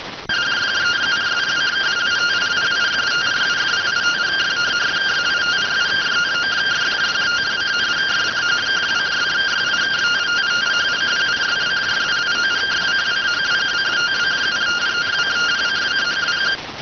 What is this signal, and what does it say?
Two-tone teleprinter signal, about 170 Hz shift.